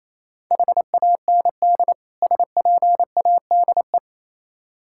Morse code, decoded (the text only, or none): HAND SPADE